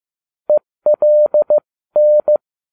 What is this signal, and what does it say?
Morse code, heard as a keyed tone, about 15 wpm, spaced as for 13 wpm, 600 Hz. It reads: ELN